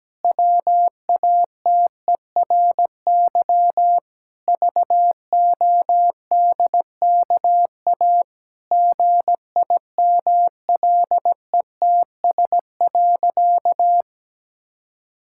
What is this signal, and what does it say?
Morse code, heard as a keyed tone, about 17 wpm, 700 Hz. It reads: WATERY VODKA GIMLETS.
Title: Pangram Challenge v2 - 17wpm